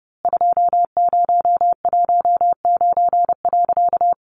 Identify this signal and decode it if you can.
Morse code, heard as a keyed tone, about 29 wpm, 700 Hz. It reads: 2019.
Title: A Sound Of Thunder 30wpm